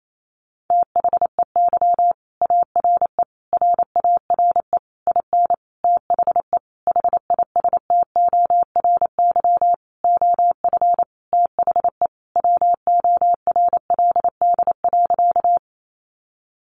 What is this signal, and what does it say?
Morse code, heard as a keyed tone, about 28 wpm, 700 Hz. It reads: THEY ARE RARE IN THE HISTORY OF THE WORLD.
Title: Sentences from Top 1100 Words 28wpm